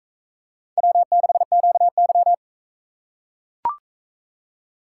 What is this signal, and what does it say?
Morse code, heard as a keyed tone, about 39 wpm, 700 Hz. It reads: W6QY